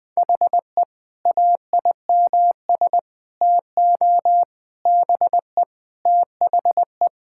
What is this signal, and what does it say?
Morse code, heard as a keyed tone, about 20 wpm, 700 Hz. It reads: HE AIMS TO BE THE
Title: Sentences with 8-Word Phrases - Spoken Beginning - 20wpm